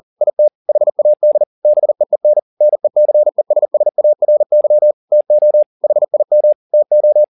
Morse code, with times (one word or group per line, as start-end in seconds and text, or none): 0.2-0.5 IT
0.7-1.4 HAD
1.6-2.4 BEEN
2.6-4.9 NECESSARY
5.1-5.6 TO
5.8-6.5 HIM
6.7-7.2 TO